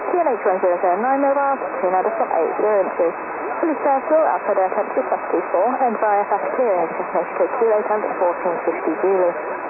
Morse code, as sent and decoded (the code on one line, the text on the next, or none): none